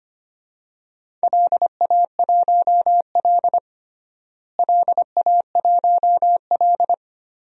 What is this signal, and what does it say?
Morse code, heard as a keyed tone, about 25 wpm, 700 Hz. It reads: LA1L LA1L
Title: DXCC-100 Call Signs - 4 Characters - Repeated 3x - 25wpm